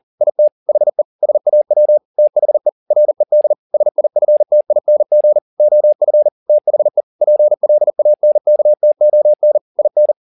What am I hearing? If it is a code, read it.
Morse code, heard as a keyed tone, about 40 wpm, 600 Hz. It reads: IT HE SAW THE RED SIFTING OF THE PLANKTON IN